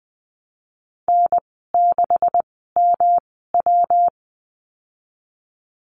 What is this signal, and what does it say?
Morse code, heard as a keyed tone, about 20 wpm, spaced as for 15 wpm, 700 Hz. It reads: N6MW
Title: 173 Lesson 28g - Number 9 - Call Signs 20wpm